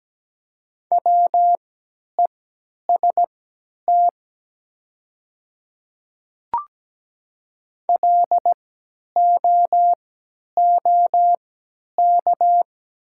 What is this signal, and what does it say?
Morse code, heard as a keyed tone, about 17 wpm, spaced as for 10 wpm, 700 Hz. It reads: WEST LOOK
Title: 3x Character Spacing - Top 500 Words - 17wpm